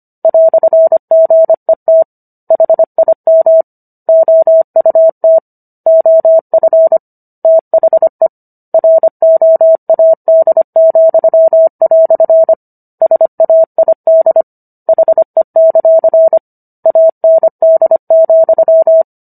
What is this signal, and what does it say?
Morse code, heard as a keyed tone, about 25 wpm, 650 Hz. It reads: "GET HIM OUT OF THE ROAD," SAID HE; AND,